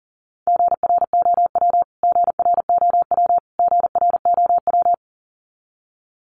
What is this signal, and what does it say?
Morse code, heard as a keyed tone, about 37 wpm, 700 Hz. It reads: GROW GROW GROW
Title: Top 500 Words Repeated 40wpm